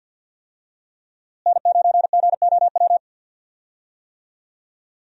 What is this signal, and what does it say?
Morse code, heard as a keyed tone, about 53 wpm, 700 Hz. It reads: N9GOW
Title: USA Call Signs - 5 Characters - Repeated 2x - 50wpm